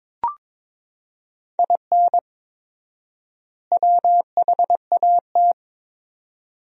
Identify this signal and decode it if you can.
Morse code, heard as a keyed tone, about 22 wpm, 700 Hz. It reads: IN WHAT
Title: 4x Word-Spacing - Sentences from Top 200 Words - 22wpm